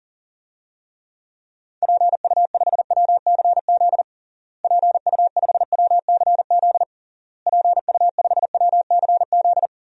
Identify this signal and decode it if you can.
Morse code, heard as a keyed tone, about 40 wpm, 700 Hz. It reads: PU5WCZ PU5WCZ PU5WCZ
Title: DXCC-100 Call Signs - 6 Characters - Repeated 3x - 40wpm